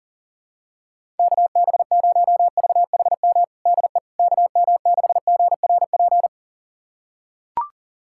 Morse code, heard as a keyed tone, about 37 wpm, 700 Hz. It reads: KB0VHM DE KM6GRP
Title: USA Callsign DE Callsign - 6 Characters - 40wpm